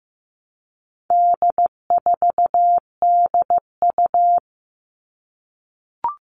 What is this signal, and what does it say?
Morse code, heard as a keyed tone, about 15 wpm, 700 Hz. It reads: D4DU
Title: All Cut Numbers - 4-Digit 15wpm